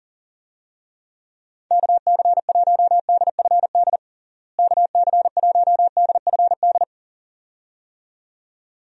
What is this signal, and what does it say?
Morse code, heard as a keyed tone, about 40 wpm, 700 Hz. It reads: KC1DFD KC1DFD
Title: USA Call Signs - 6 Characters - Repeated 2x - 40wpm